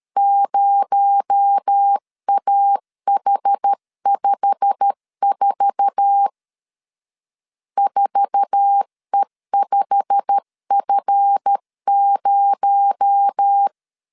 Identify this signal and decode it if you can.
Morse code, heard as a keyed tone, about 13 wpm, 800 Hz. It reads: 0AH54 4E5F0
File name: LEZIONI / 8 YJ45 / 8 YJ45 12wpm 2151